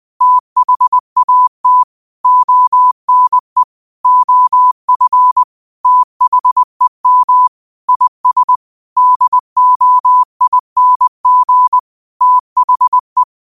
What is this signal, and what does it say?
Morse code, heard as a keyed tone, about 20 wpm, 1000 Hz. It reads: THAT ONE OF THEM IS DOING THE
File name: qotdfiles / 20wpm